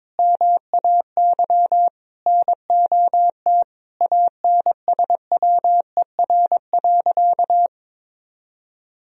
Morse code, heard as a keyed tone, about 21 wpm, 700 Hz. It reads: MAY NOT ANSWER.
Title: Sentences from Top 1000 Words - Review Part 4 - 22wpm